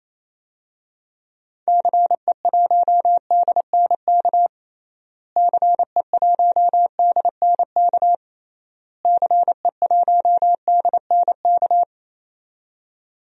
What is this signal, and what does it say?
Morse code, heard as a keyed tone, about 28 wpm, 700 Hz. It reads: CE1DNK CE1DNK CE1DNK
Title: DXCC-100 Call Signs - 6 Characters - Repeated 3x - 28wpm